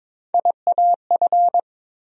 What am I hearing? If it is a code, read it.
Morse code, heard as a keyed tone, about 22 wpm, 700 Hz. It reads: IAF
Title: USA Call Sign Postfixes - 3 Letters 22wpm